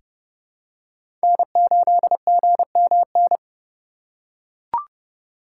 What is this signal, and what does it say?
Morse code, heard as a keyed tone, about 30 wpm, 700 Hz. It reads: N8GMN
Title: USA Call Signs - 2x Character Spacing - 30wpm